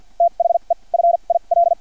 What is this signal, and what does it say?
Morse code, heard as a keyed tone, about 50 wpm, spaced as for 37 wpm, 680 Hz. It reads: THE FIL
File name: archived / CRSnet